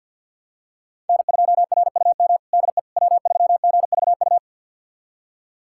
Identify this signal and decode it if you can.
Morse code, heard as a keyed tone, about 45 wpm, 690 Hz. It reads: N2RUM DE W3GVU
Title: USA Callsign DE Callsign - 5 Characters - 50wpm